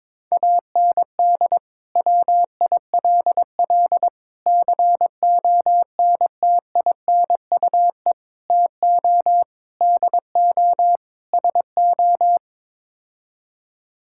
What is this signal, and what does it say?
Morse code, heard as a keyed tone, about 22 wpm, 700 Hz. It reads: AND WILL CONTINUE TO DO SO
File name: practice / 22wpm / Sets of 6 Words - 22wpm